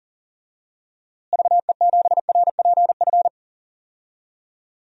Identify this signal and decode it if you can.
Morse code, heard as a keyed tone, about 40 wpm, 700 Hz. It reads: VE7RPF
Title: DXCC-100 Call Signs - 6 Characters - Repeated 2x - 40wpm